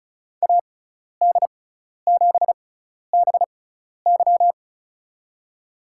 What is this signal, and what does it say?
Morse code, heard as a keyed tone, about 36 wpm, spaced as for 12 wpm, 700 Hz. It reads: AD7BY
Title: USA Call Signs - 5 Characters - 6x Character Spacing - 35wpm